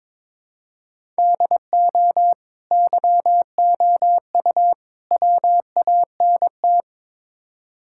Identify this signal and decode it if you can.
Morse code, heard as a keyed tone, about 22 wpm, 700 Hz. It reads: DO YOU WANT
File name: practice / 22wpm / Build A Phrase - 5-Word Phrases - 22wpm